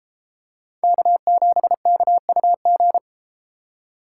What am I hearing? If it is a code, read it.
Morse code, heard as a keyed tone, about 31 wpm, 700 Hz. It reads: K7KUG